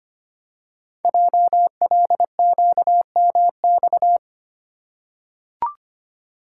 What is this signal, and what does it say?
Morse code, heard as a keyed tone, about 25 wpm, 700 Hz. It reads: JLQMX
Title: Code Groups - 5 Letters - 25wpm